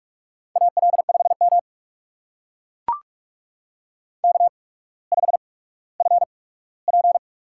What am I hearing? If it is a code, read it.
Morse code, heard as a keyed tone, about 44 wpm, spaced as for 19 wpm, 700 Hz. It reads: AL5M K5FP
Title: USA Call Signs - 8x Character Spacing - 45wpm